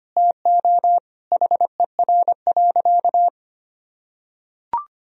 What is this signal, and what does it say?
Morse code, heard as a keyed tone, about 25 wpm, 700 Hz. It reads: TO HER.
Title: Sentences from Top 1000 Words - Review Part 2 - 25wpm